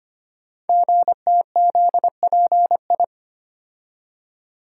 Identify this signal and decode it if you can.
Morse code, heard as a keyed tone, about 25 wpm, 700 Hz. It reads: GTZPI
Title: Code Groups - 5 Letters - 25wpm